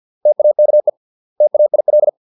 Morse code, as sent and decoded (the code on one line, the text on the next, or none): - .- -.- . / - .- .. .-..
TAKE TAIL